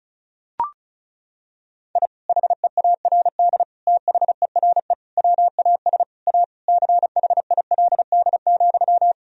Morse code, heard as a keyed tone, about 36 wpm, 700 Hz. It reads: I HEARD THERE WAS A CHILD,
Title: Sentences from Top 1000 Words - Review Part 2 - 35wpm